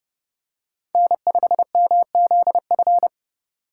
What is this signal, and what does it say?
Morse code, heard as a keyed tone, about 30 wpm, 700 Hz. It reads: N5MZF
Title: USA Call Signs - 5 Characters - 2x Character Spacing - 30wpm